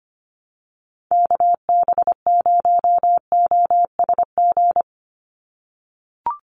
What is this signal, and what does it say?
Morse code, heard as a keyed tone, about 25 wpm, 700 Hz. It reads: KB0OSG
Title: USA Call Signs 25wpm